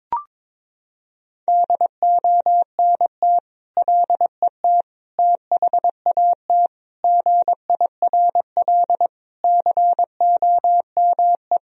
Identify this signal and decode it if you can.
Morse code, heard as a keyed tone, about 22 wpm, 700 Hz. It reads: DONT LET THAT GIRL COME
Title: Sentences from Top 1000 Words - Review Part 1 - 22wpm